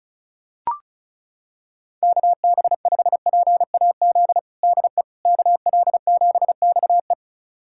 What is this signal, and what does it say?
Morse code, heard as a keyed tone, about 36 wpm, 700 Hz. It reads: KB5PAZ DE KL7XE
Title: USA Callsign DE Callsign 35wpm